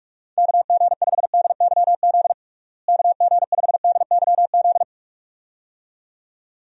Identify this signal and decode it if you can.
Morse code, heard as a keyed tone, about 46 wpm, 700 Hz. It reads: KG5DYZ KG5DYZ
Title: USA Call Signs - 6 Characters - Repeated 2x - 45wpm